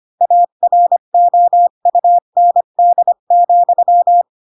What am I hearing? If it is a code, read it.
Morse code, heard as a keyed tone, about 25 wpm, 700 Hz. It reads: AROUND,
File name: Pinocchio / Pinocchio ch8 20 25 ik1hge